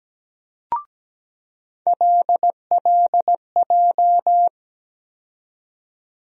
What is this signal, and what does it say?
Morse code, heard as a keyed tone, about 17 wpm, 700 Hz. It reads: LLJ
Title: USA Call Sign Postfixes - 3 Letters 17wpm